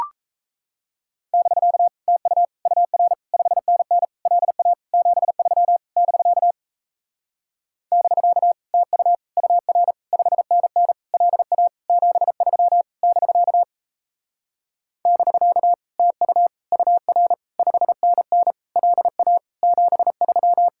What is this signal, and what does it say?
Morse code, heard as a keyed tone, about 37 wpm, 700 Hz. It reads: <BK> TU UR 5NN LA 73 <BK> <BK> TU UR 5NN LA 73 <BK> <BK> TU UR 5NN LA 73